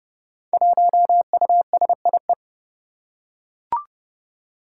Morse code, heard as a keyed tone, about 30 wpm, 700 Hz. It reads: 1USIE
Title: Five Challenging Letter-Number Combinations 30wpm